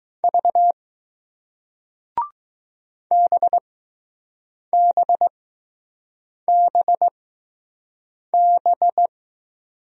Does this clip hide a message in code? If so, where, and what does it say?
Yes: Morse code from 0.2 to 9.1 s, heard as a keyed tone, about 21 wpm, spaced as for 6 wpm, 700 Hz. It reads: V BBBB